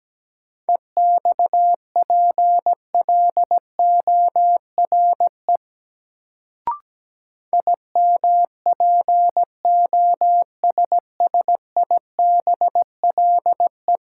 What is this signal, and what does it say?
Morse code, heard as a keyed tone, about 17 wpm, 700 Hz. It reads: EXPLORE IMPOSSIBLE
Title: Sentences from Top 1100 Words - New Word Drill - 17wpm